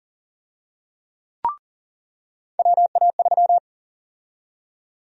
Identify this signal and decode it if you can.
Morse code, heard as a keyed tone, about 40 wpm, 700 Hz. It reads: WA3
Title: USA Call Sign Prefixes - 40wpm